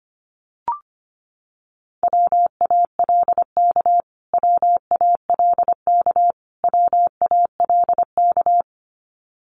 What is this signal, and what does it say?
Morse code, heard as a keyed tone, about 25 wpm, 700 Hz. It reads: WALK WALK WALK
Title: Top 500 Words Repeated 25wpm